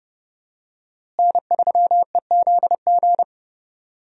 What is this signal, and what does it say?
Morse code, heard as a keyed tone, about 30 wpm, 700 Hz. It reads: N3EZG
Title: USA Call Signs - 5 Characters - Repeated 3x - 30wpm